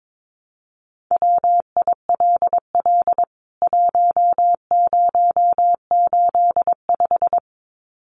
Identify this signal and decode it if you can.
Morse code, heard as a keyed tone, about 22 wpm, 700 Hz. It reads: WILL 1085
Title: CWT Exchange 22wpm